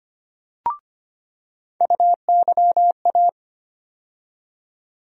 Morse code, heard as a keyed tone, about 25 wpm, 700 Hz. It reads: UYA
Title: USA Call Sign Postfixes - 3 Letters 25wpm